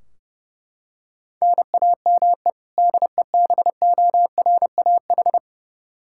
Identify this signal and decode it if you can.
Morse code, heard as a keyed tone, about 30 wpm, 700 Hz. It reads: NAME DEBORAH
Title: Practice QSO - Exchange 1 - Beginning - 30wpm